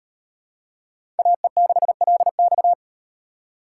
Morse code, heard as a keyed tone, about 34 wpm, 700 Hz. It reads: AE6LX